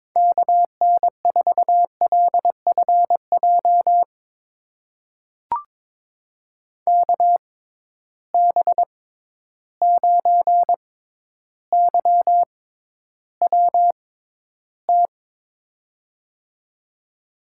Morse code, heard as a keyed tone, about 22 wpm, spaced as for 11 wpm, 700 Hz. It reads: KN4LFJ KB9YWT
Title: USA Call Signs - 6 Characters - 6x Character Spacing - 22wpm